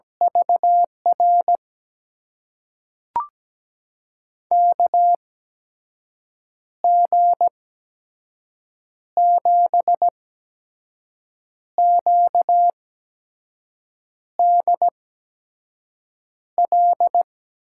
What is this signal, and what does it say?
Morse code, heard as a keyed tone, about 17 wpm, spaced as for 5 wpm, 700 Hz. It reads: VR KG7QDL
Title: USA Call Signs - 6 Characters - 8x Character Spacing - 17wpm